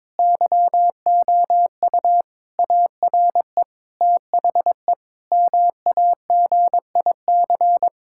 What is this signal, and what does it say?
Morse code, heard as a keyed tone, about 22 wpm, 700 Hz. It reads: YOU ARE THE MAGIC
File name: Difficult Sentences with Hints - 22wpm